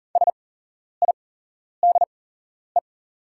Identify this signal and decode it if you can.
Morse code, heard as a keyed tone, about 41 wpm, spaced as for 11 wpm, 690 Hz. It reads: SIDE
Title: 8x Character Spacing - Top 500 Words - 40wpm